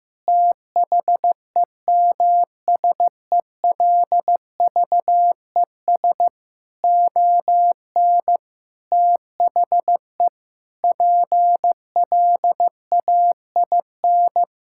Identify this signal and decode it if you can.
Morse code, heard as a keyed tone, about 15 wpm, 700 Hz. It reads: THEMSELVES ON THE PLAIN
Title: Sentences from Top 1000 Words - Review Part 5 - 15wpm